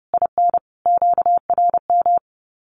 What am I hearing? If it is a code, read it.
Morse code, heard as a keyed tone, about 30 wpm, 700 Hz. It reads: IN QRM
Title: Common QSO Phrases 30wpm